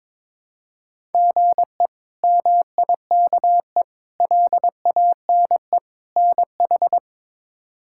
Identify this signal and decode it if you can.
Morse code, heard as a keyed tone, about 21 wpm, 700 Hz. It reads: GE MIKE LANE NH
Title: K1USN SST QSO Practice - 22wpm